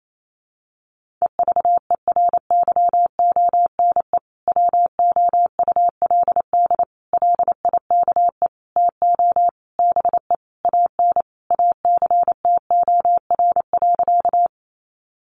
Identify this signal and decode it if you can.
Morse code, heard as a keyed tone, about 28 wpm, 700 Hz. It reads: EVERYONE WOULD LIKE TO BE AN ACTOR.
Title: Sentences from Top 1500 Words 28wpm